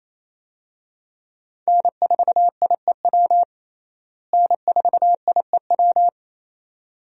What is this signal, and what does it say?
Morse code, heard as a keyed tone, about 28 wpm, 700 Hz. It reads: N4IEW N4IEW
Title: USA Call Signs - 5 Characters - Repeated 2x - 28wpm